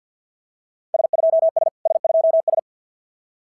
Morse code, heard as a keyed tone, about 52 wpm, 690 Hz. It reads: S2S S2S
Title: SOTA QSO Practice 50wpm